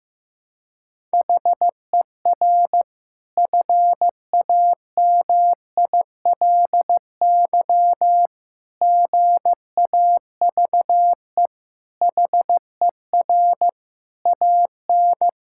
Morse code, heard as a keyed tone, about 14 wpm, 700 Hz. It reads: HER FAMILY GAVE HER AN
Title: Sentences from Top 1000 Words - Review Part 5 - 15wpm